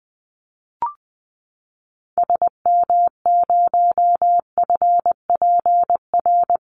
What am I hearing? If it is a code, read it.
Morse code, heard as a keyed tone, about 20 wpm, 700 Hz. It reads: SM0FPR
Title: Call Signs Repeated 2x - 20wpm